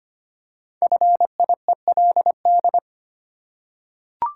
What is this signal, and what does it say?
Morse code, heard as a keyed tone, about 25 wpm, 700 Hz. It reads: FIELD
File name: 5 Letter Words - First 3 Letters Given - 25wpm